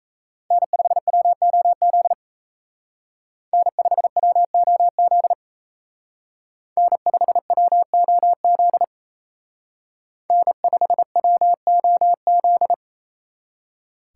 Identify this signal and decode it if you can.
Morse code, heard as a keyed tone, about 32 wpm, 700 Hz. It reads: N5WOZ N5WOZ N5WOZ N5WOZ
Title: Speed Racing - USA 1x3 Call Signs 28wpm